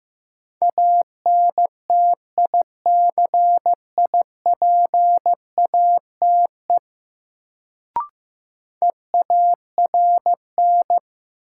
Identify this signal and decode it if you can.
Morse code, heard as a keyed tone, about 16 wpm, 700 Hz. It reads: ANTICIPATE EARN
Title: Sentences from Top 1500 Words - New Word Drill - 15wpm